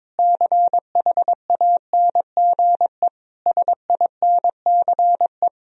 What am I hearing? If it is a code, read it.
Morse code, heard as a keyed tone, about 21 wpm, 700 Hz. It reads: CHANGE SINCE
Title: Sentences from Top 1000 Words - Review Part 1 - 22wpm